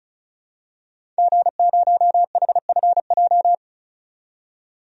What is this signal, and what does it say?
Morse code, heard as a keyed tone, about 36 wpm, 700 Hz. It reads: G0HFJ
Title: DXCC-100 Call Signs - 5 Characters - 35wpm